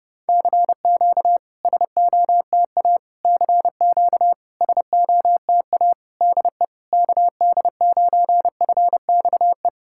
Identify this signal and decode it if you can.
Morse code, heard as a keyed tone, about 30 wpm, 660 Hz. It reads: CQ SOTA CQ SOTA DE KD9FXE
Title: SOTA QSO Practice 30wpm